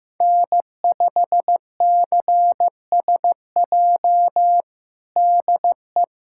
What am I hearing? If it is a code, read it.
Morse code, heard as a keyed tone, about 15 wpm, 700 Hz. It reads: N5CSJ DE
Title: USA Callsign DE Callsign - 5 Characters - 15wpm